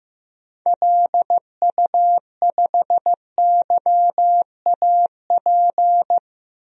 Morse code, heard as a keyed tone, about 15 wpm, 700 Hz. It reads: LU5YAP
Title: DXCC-100 Call Signs - 6 Characters - Repeated 3x - 15wpm